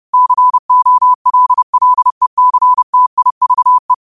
Morse code, heard as a keyed tone, about 30 wpm, 1000 Hz. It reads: COLLECTIVE
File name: 30wpm